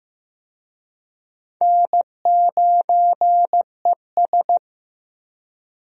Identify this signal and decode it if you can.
Morse code, heard as a keyed tone, about 15 wpm, 700 Hz. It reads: N9ES
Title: USA Call Signs - 4 Characters - Repeated 3x - 15wpm